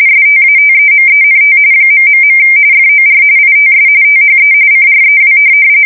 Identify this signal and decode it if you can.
RTTY, about 45.45 baud. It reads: STATEMENT ... FOR ITS THE CREW
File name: fsk170 / USS-Missouri